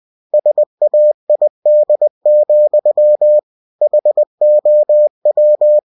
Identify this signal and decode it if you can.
Morse code, heard as a keyed tone, about 20 wpm, 600 Hz. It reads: SAID, HOW
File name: cw-books / winnie-the-pooh / winnie-20wpm-page-0034